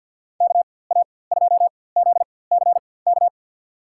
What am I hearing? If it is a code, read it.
Morse code, heard as a keyed tone, about 47 wpm, spaced as for 23 wpm, 700 Hz. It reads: KA2ZCK
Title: USA Call Signs - 6 Characters - 4x Character Spacing - 50wpm